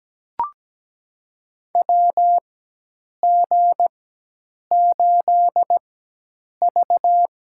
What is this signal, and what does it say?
Morse code, heard as a keyed tone, about 17 wpm, spaced as for 8 wpm, 700 Hz. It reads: WG8V
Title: USA Call Signs - 4x Character Spacing - 17wpm